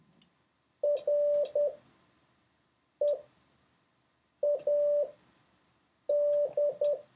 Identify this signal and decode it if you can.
Morse code, heard as a keyed tone, about 10 wpm, spaced as for 5 wpm, 580 Hz. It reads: READ